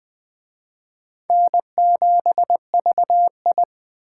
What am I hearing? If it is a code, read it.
Morse code, heard as a keyed tone, about 21 wpm, 700 Hz. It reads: N7VI
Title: Speed Racing - USA Call Signs 17wpm